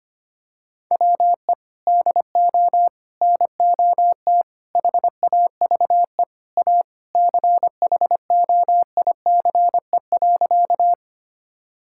Morse code, heard as a keyed tone, about 25 wpm, 700 Hz. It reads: WE DO NOT HAVE A CHOICE.
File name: practice / 25wpm / Sentences with 3-Word Phrases - 25wpm